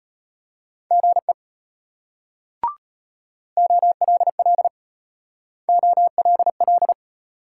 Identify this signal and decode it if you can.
Morse code, heard as a keyed tone, about 34 wpm, 700 Hz. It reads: GE OLL OLL